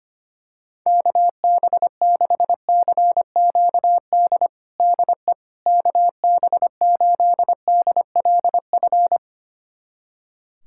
Morse code, heard as a keyed tone, about 25 wpm, 700 Hz. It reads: KB6CQD DE KB8DLF
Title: USA Callsign DE Callsign 25wpm